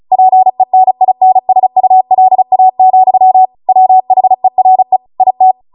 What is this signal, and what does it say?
Morse code, heard as a keyed tone, about 36 wpm, 750 Hz. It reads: PENINSULA, WHERE IT